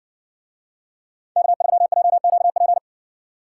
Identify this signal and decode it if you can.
Morse code, heard as a keyed tone, about 60 wpm, 700 Hz. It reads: K3JQP